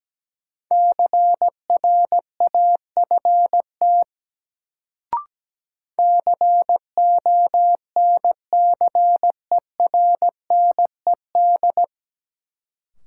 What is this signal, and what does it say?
Morse code, heard as a keyed tone, about 17 wpm, 700 Hz. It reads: CRAFT CONCERNED
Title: Sentences from Top 1000 Words - Review Part 6 - New Word Drill - 17wpm